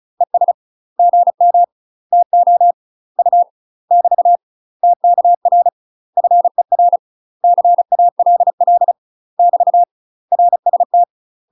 Morse code, heard as a keyed tone, about 36 wpm, 700 Hz. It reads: ES GM TO U <BT> TKR FER CALL <BT> RST
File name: UK MASTER QSOs TRAINING/35WPM